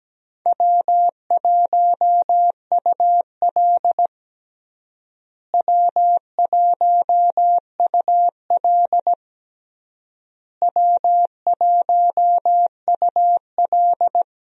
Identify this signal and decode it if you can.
Morse code, heard as a keyed tone, about 17 wpm, 700 Hz. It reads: W1UL W1UL W1UL
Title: USA Call Signs - 4 Characters - Repeated 3x - 17wpm